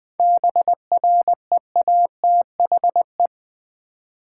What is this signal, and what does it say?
Morse code, heard as a keyed tone, about 20 wpm, 700 Hz. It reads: BREATHE
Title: 7 Letter Words - First 3 Letters Given - 20wpm